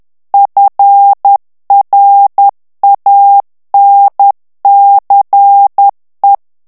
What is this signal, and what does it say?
Morse code, heard as a keyed tone, about 10 wpm, 800 Hz. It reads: FRANCE
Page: Français - texte en clair 1 - vitesse : 10 mots minute